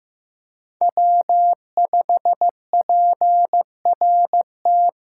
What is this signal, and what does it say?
Morse code, heard as a keyed tone, about 15 wpm, 700 Hz. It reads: W5PRT
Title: USA Callsign DE Callsign - 5 Characters - 15wpm